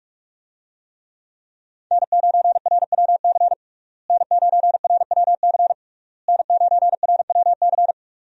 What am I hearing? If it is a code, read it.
Morse code, heard as a keyed tone, about 43 wpm, 700 Hz. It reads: N9RWC N9RWC N9RWC
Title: USA Call Signs - 5 Characters - Repeated 3x - 45wpm